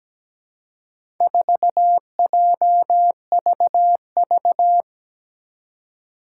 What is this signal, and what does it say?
Morse code, heard as a keyed tone, about 17 wpm, 700 Hz. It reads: 4JVV
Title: Four Challenging Letter-Number Combinations 17wpm